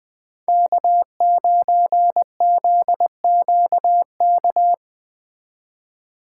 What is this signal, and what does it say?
Morse code, heard as a keyed tone, about 21 wpm, 700 Hz. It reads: K9ZQK